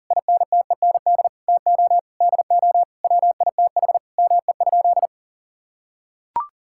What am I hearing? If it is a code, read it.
Morse code, heard as a keyed tone, about 40 wpm, 700 Hz. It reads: INTEND TO DO WITH ME?
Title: Sentences from Top 1600 Words 40wpm